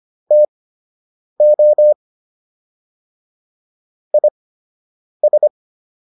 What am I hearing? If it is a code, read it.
Morse code, heard as a keyed tone, about 25 wpm, spaced as for 8 wpm, 600 Hz. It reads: TO IS